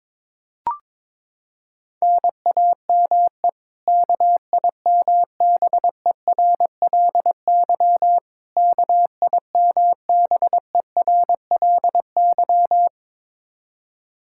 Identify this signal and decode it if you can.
Morse code, heard as a keyed tone, about 21 wpm, 700 Hz. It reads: NAME KIMBERLY KIMBERLY
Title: Practice QSO - Exchange 1 - Beginning - 22wpm